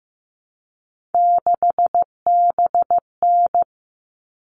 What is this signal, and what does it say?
Morse code, heard as a keyed tone, about 15 wpm, 700 Hz. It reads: 6BN
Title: All Cut Numbers - 3-Digit 15wpm